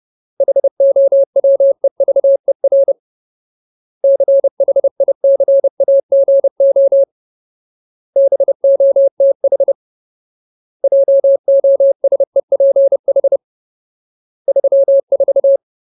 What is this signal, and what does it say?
Morse code, heard as a keyed tone, about 30 wpm, 550 Hz. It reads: HOWEVER CHICAGO BOTH JOSEPH 34
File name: Ultimate Word List / 30WPM Ultimate